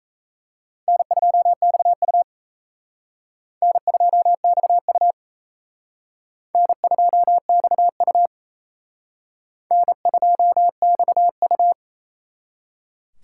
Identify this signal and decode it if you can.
Morse code, heard as a keyed tone, about 35 wpm, 700 Hz. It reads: N2XU N2XU N2XU N2XU
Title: Speed Racing - USA 1x2 Call Signs 28wpm